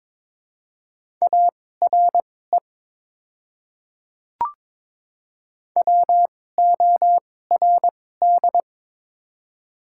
Morse code, heard as a keyed tone, about 23 wpm, spaced as for 16 wpm, 700 Hz. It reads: ARE WORD
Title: 2x Character Spacing - Top 100 Words - 22wpm